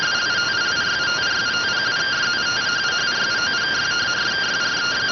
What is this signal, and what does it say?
Two-tone teleprinter signal, about 170 Hz shift.